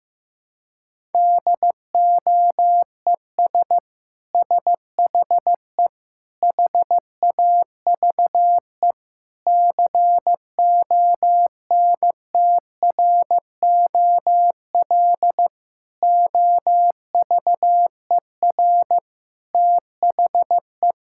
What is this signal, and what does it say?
Morse code, heard as a keyed tone, about 15 wpm, 700 Hz. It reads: DOES SHE HAVE CONTROL OVER THE
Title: Sentences from Top 1000 Words - Review Part 5 - 15wpm